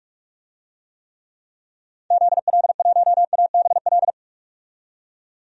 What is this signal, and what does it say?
Morse code, heard as a keyed tone, about 47 wpm, 700 Hz. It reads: ZL1ABL
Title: DXCC-100 Call Signs - 6 Characters - 45wpm